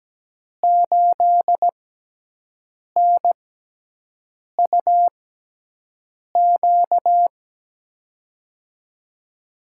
Morse code, heard as a keyed tone, about 17 wpm, spaced as for 6 wpm, 700 Hz. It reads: 8NUQ
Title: USA Call Signs - 6 Characters - 6x Character Spacing - 17wpm